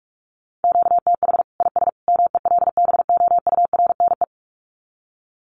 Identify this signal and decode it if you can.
Morse code, heard as a keyed tone, about 44 wpm, 700 Hz. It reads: QTH IS MELBOURNE
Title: Common QSO Phrases 45wpm